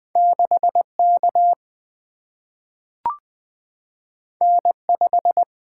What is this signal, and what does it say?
Morse code, heard as a keyed tone, about 20 wpm, 700 Hz. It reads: 6K N5
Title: Two Letter-Number Combinations - 20wpm